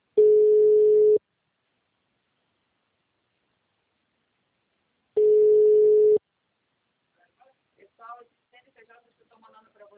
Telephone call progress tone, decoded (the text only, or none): ringback tone